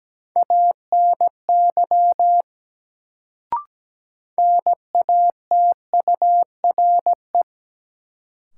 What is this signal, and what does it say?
Morse code, heard as a keyed tone, about 17 wpm, 700 Hz. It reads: ANY NATURE
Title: Sets of 3 Words - New Word Drill - 17wpm